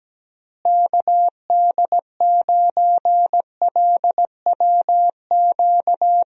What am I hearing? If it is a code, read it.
Morse code, heard as a keyed tone, about 17 wpm, 700 Hz. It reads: KD9LWQ
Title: USA Call Signs - 6 Characters - Repeated 3x - 17wpm